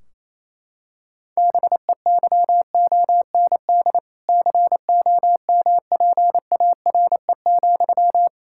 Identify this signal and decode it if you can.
Morse code, heard as a keyed tone, about 28 wpm, 700 Hz. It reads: BEYOND COMPARE,
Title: Sentences from Top 1600 Words 28wpm